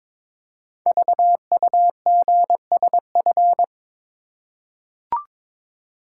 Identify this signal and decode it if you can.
Morse code, heard as a keyed tone, about 22 wpm, 700 Hz. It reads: VUGSF
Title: Code Groups - 5 Letters - 22wpm